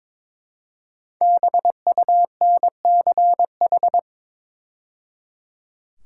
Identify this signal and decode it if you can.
Morse code, heard as a keyed tone, about 22 wpm, 700 Hz. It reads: BUNCH
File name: 5 Letter Words - First 3 Letters Given - 22wpm